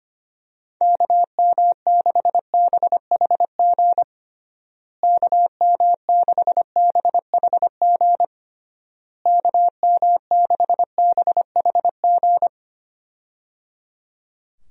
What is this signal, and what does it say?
Morse code, heard as a keyed tone, about 25 wpm, 700 Hz. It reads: KM6BHG KM6BHG KM6BHG
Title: USA Call Signs - 6 Characters - Repeated 3x - 25wpm